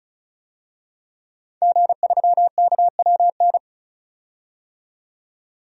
Morse code, heard as a keyed tone, about 36 wpm, 700 Hz. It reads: G3KWN
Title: DXCC-100 Call Signs - 5 Characters - 35wpm